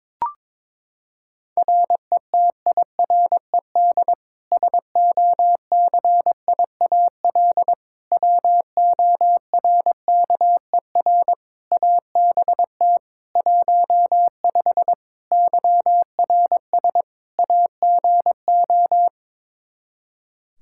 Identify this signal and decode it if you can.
Morse code, heard as a keyed tone, about 22 wpm, 700 Hz. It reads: RETIRED SOCIAL WORKER ABT 15 YRS AGO
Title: K1USN SST QSO Practice 22wpm